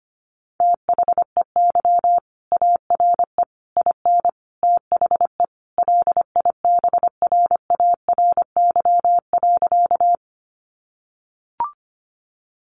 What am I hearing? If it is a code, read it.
Morse code, heard as a keyed tone, about 25 wpm, 700 Hz. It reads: THEY ARE IN THE LIBRARY.
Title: Sentences from Top 600 Words 25wpm